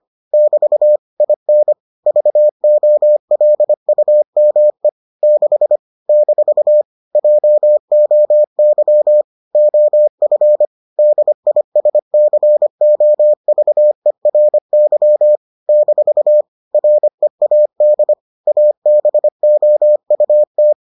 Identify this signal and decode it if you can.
Morse code, heard as a keyed tone, about 25 wpm, 600 Hz. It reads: <BT> IN VOLUME 6 - JOY OF DISCOVERY - READ ABOUT